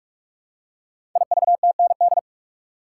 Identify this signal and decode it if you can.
Morse code, heard as a keyed tone, about 44 wpm, 700 Hz. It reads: IVTND
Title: Code Groups - 5 Letters - 45wpm